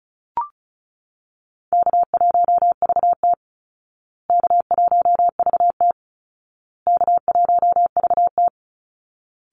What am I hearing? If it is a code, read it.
Morse code, heard as a keyed tone, about 34 wpm, 700 Hz. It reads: K1VT K1VT K1VT
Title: Easy Call Signs Repeated 3x - 35wpm